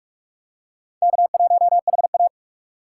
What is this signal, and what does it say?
Morse code, heard as a keyed tone, about 44 wpm, 700 Hz. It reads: K1HA
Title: USA Call Signs - 4 Characters - 6x Character Spacing - 45wpm